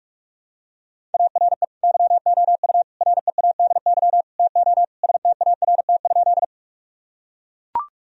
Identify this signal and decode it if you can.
Morse code, heard as a keyed tone, about 42 wpm, 700 Hz. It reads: ARE YOU READY TO START?
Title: Sentences from Top 1000 Words - Review Part 1 - 45wpm